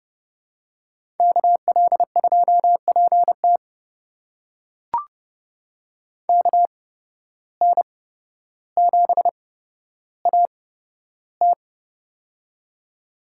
Morse code, heard as a keyed tone, about 30 wpm, spaced as for 13 wpm, 700 Hz. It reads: KL2PT KN7AT
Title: USA Call Signs - 5 Characters - 8x Character Spacing - 30wpm